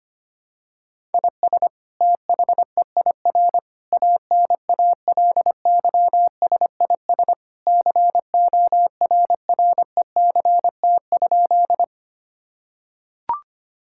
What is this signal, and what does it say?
Morse code, heard as a keyed tone, about 25 wpm, 700 Hz. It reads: IS THEIR ANALYSIS CORRECT?
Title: Sentences from Top 1000 Words - Review Part 5 - 25wpm